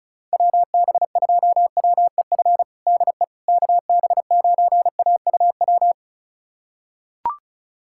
Morse code, heard as a keyed tone, about 34 wpm, 700 Hz. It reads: WB2WEF DE KB9AUW
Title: USA Callsign DE Callsign 35wpm